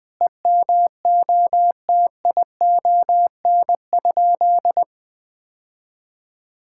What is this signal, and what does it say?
Morse code, heard as a keyed tone, about 20 wpm, 700 Hz. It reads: EMOTION?
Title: Sentences from Top 1600 Words 20wpm